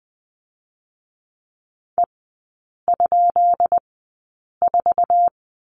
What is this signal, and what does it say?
Morse code, heard as a keyed tone, about 20 wpm, 700 Hz. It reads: E ? 4
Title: 133 Lesson 22e - Question Mark - Three Characters 20wpm